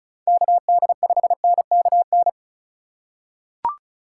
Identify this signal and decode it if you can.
Morse code, heard as a keyed tone, about 36 wpm, 700 Hz. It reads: KD5NKN
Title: USA Call Signs - 3x Character Spacing - 35wpm